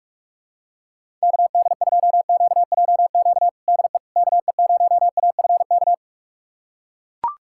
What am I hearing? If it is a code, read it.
Morse code, heard as a keyed tone, about 42 wpm, 700 Hz. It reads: KD2QJQ DE KE0AFK
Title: USA Callsign DE Callsign 45wpm